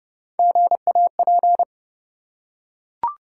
Morse code, heard as a keyed tone, about 30 wpm, 700 Hz. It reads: GAP
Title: Three Letter Words - 30wpm